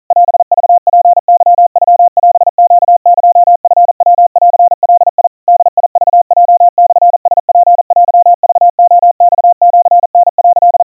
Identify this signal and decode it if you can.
Morse code (Japanese Wabun code), heard as a keyed tone, about 40 wpm, 700 Hz. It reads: オクツケノオネエチヤンガ ボクヲモラツテクレマシタ」
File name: kawa / Morse sample / kamyu 150 cpm 700 Hz Crispy